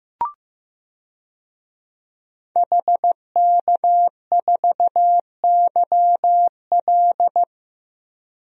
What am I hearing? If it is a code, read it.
Morse code, heard as a keyed tone, about 15 wpm, 700 Hz. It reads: HK4YL
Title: DXCC-100 Call Signs - 5 Characters - Repeated 3x - 15wpm